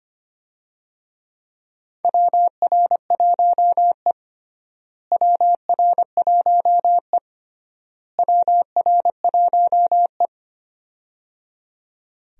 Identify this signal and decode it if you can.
Morse code, heard as a keyed tone, about 25 wpm, 700 Hz. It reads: WR1E WR1E WR1E
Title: USA Call Signs - 4 Characters - Repeated 3x - 25wpm